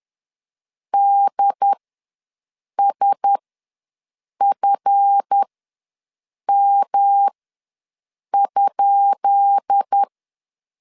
Morse code, heard as a keyed tone, about 10 wpm, spaced as for 6 wpm, 800 Hz. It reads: DSFM?
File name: LEZIONI / 10 SIGNS / 10B INT ws 10wpm 4134